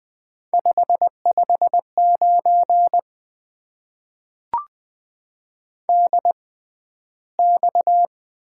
Morse code, heard as a keyed tone, about 20 wpm, spaced as for 12 wpm, 700 Hz. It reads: 559 DX
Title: Top 90 QSO Elements - 6x Character Spacing - 20wpm